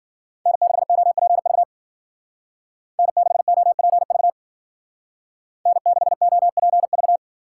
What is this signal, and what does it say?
Morse code, heard as a keyed tone, about 49 wpm, 700 Hz. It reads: N6OPV N6OPV N6OPV
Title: Speed Racing - USA 1x3 Call Signs 40wpm